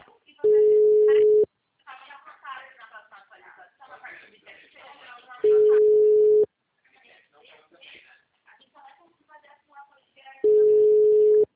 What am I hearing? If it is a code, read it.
Ringback tone (Europe, 425 Hz), with speech.